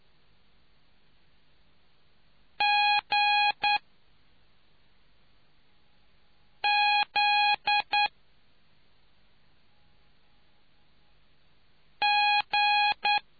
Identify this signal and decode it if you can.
Morse code, heard as a keyed tone, about 9 wpm, 830 Hz. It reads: G Z G